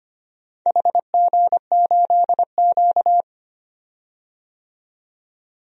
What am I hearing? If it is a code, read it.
Morse code, heard as a keyed tone, about 25 wpm, 700 Hz. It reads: HG8Q
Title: DXCC-100 Call Signs - 4 Characters - 25wpm